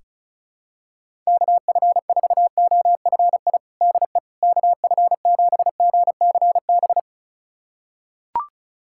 Morse code, heard as a keyed tone, about 34 wpm, 700 Hz. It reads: KF4OFI DE KF7GCB
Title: USA Callsign DE Callsign - 6 Characters - 35wpm